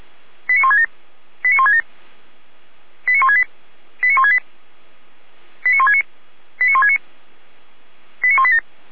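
Five-tone selective call (ZVEI), heard as quick runs of five tones, tones of about 80 ms. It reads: 89167 89167 89168 89168 89169 89169 89187
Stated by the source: Five-tone ZVEI selective call received by SDR, from the Feuerwehr Orlamünde (fire brigade) audio server.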